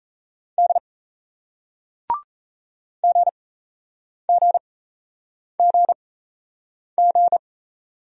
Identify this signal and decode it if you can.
Morse code, heard as a keyed tone, about 34 wpm, spaced as for 8 wpm, 700 Hz. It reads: D GGGG
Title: Speed-Racing Single Letters 28wpm